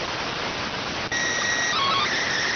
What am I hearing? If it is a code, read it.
SSTV (slow-scan television), PD 290.